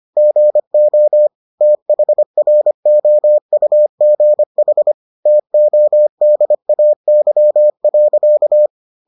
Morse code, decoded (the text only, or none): GO THROUGH TODAY.